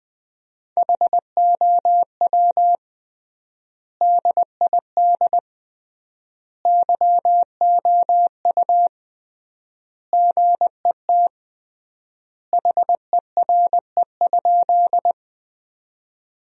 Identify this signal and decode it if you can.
Morse code, heard as a keyed tone, about 21 wpm, 700 Hz. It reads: HOW DID YOU GET HERE?
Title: 3x Word-Spacing - Sentences from Top 200 Words - 20wpm